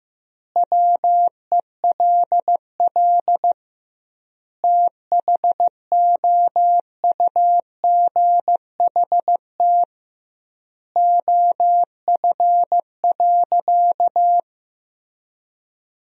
Morse code, heard as a keyed tone, about 15 wpm, 700 Hz. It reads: WELL THOUGHT OF.
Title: 2x Word-Spacing - Sentences from Top 300 Words - 15wpm